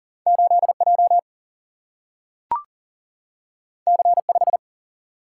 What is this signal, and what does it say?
Morse code, heard as a keyed tone, about 40 wpm, 700 Hz. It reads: 8J C5
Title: Single-Letter Single-Number 40wpm